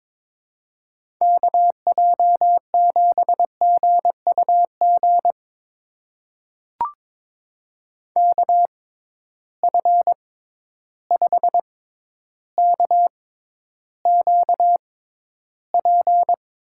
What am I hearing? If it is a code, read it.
Morse code, heard as a keyed tone, about 22 wpm, spaced as for 11 wpm, 700 Hz. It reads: KJ7GUG KF5KQP
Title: USA Call Signs - 6 Characters - 6x Character Spacing - 22wpm